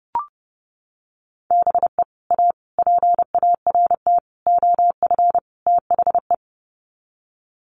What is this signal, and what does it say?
Morse code, heard as a keyed tone, about 30 wpm, 700 Hz. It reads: BE A PART OF THE
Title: Sets of 5 Words Encore 30wpm